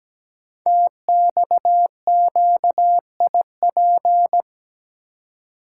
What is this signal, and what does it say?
Morse code, heard as a keyed tone, about 17 wpm, 700 Hz. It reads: TXQIP
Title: Code Groups - 5 Letters - 17wpm